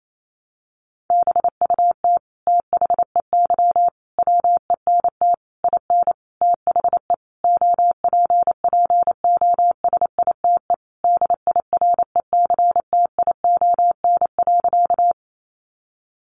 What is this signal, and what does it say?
Morse code, heard as a keyed tone, about 28 wpm, 700 Hz. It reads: BUT THEY WENT IN THE OPPOSITE DIRECTION.
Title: Sentences from Top 1300 Words 28wpm